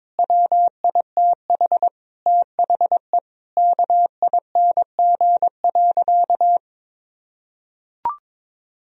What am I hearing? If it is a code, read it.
Morse code, heard as a keyed tone, about 22 wpm, 700 Hz. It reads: WITH THE KING.
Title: Sentences from Top 1000 Words - Review Part 3 - 22wpm